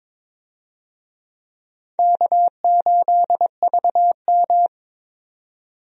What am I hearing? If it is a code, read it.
Morse code, heard as a keyed tone, about 22 wpm, 700 Hz. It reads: K8VM
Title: USA Call Signs - 4 Characters - 22wpm